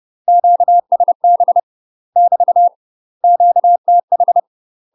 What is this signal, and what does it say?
Morse code, heard as a keyed tone, about 30 wpm, 700 Hz. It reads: QSB <BT> QTH
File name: UK MASTER QSOs TRAINING / 30WPM / UK MASTER QSOs 0034